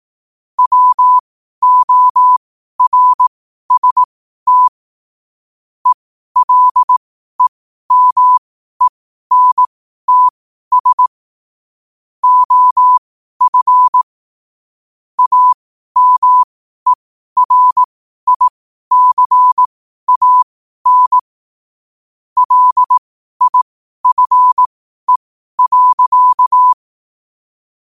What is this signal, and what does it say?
Morse code, heard as a keyed tone, about 18 wpm, spaced as for 12 wpm, 1000 Hz. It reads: WORST ELEMENTS OF AMERICAN LIFE.
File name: qotdfiles / 12wpm